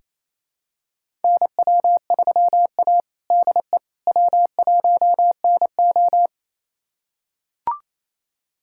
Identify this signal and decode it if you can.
Morse code, heard as a keyed tone, about 27 wpm, 690 Hz. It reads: NW3A DE W1NO
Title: USA Callsign DE Callsign - 4 Characters - 28wpm